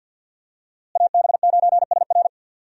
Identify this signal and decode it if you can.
Morse code, heard as a keyed tone, about 52 wpm, 700 Hz. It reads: AB9SR
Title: USA Call Signs - 5 Characters - Repeated 2x - 50wpm